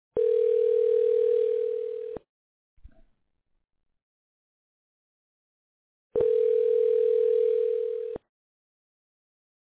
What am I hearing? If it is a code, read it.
Ringback tone (North America), 440 + 480 Hz.